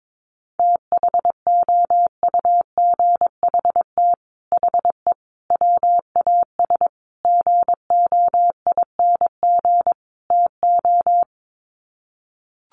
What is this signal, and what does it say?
Morse code, heard as a keyed tone, about 22 wpm, 700 Hz. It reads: THOUGHT HE WAS GOING TO
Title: Sets of 5 Words Encore2 22wpm